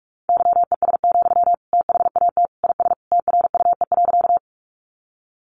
Morse code, heard as a keyed tone, about 44 wpm, 700 Hz. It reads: YES, THAT IS TRUE.
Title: Sentences from Top 500 Words 45wpm